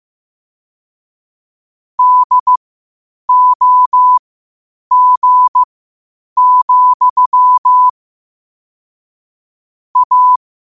Morse code, heard as a keyed tone, about 15 wpm, spaced as for 8 wpm, 1000 Hz. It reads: DOG, A
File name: qotdfiles / 8wpm